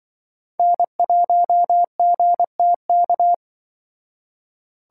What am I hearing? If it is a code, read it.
Morse code, heard as a keyed tone, about 24 wpm, 700 Hz. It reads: N1GTK
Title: Speed Racing - USA 1x3 Call Signs 20wpm